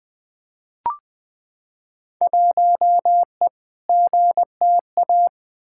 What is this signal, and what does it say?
Morse code, heard as a keyed tone, about 20 wpm, 700 Hz. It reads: 1E GTA
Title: Field Day Exchanges - 20wpm